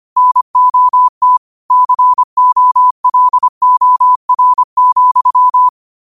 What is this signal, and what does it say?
Morse code, heard as a keyed tone, about 25 wpm, 1000 Hz. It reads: NOT COLOR,
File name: qotdfiles / 25wpm